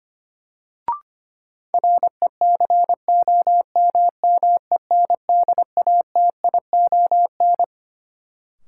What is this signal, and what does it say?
Morse code, heard as a keyed tone, about 25 wpm, 700 Hz. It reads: RECOMMENDATION
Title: Sentences from Top 1400 Words - New Word Drill Alt - 25wpm